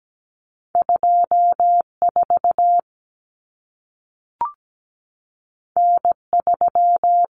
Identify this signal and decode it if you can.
Morse code, heard as a keyed tone, about 17 wpm, 700 Hz. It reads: 24 N3
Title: Common Cut Numbers - 2-Digit 17wpm